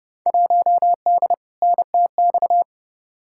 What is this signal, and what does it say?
Morse code, heard as a keyed tone, about 30 wpm, 700 Hz. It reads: 1D NTX
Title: Field Day Exchanges - 30wpm